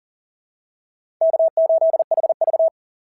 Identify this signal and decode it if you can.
Morse code, heard as a keyed tone, about 42 wpm, 690 Hz. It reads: K8HV
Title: SOTA QSO Practice 40wpm